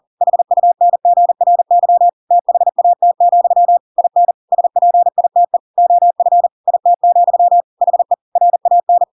Morse code, heard as a keyed tone, about 40 wpm, 700 Hz. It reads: HUNGRY THAT, IN SPITE OF IT, HE RAN